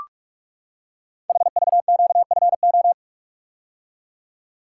Morse code, heard as a keyed tone, about 44 wpm, 700 Hz. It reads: HVQFO
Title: Code Groups - 5 Letters - 45wpm